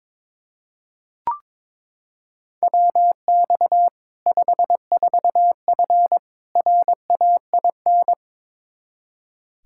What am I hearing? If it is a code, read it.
Morse code, heard as a keyed tone, about 21 wpm, 700 Hz. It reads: WX 54F RAIN
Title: Practice QSO - Exchange 2 - Info - 22wpm